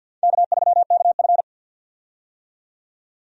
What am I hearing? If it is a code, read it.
Morse code, heard as a keyed tone, about 52 wpm, 700 Hz. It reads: K3KF
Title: USA Call Signs - 4 Characters - 50wpm